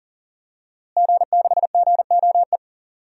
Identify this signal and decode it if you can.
Morse code, heard as a keyed tone, about 40 wpm, 700 Hz. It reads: G6GOE
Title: DXCC-100 Call Signs - 5 Characters - Repeated 2x - 40wpm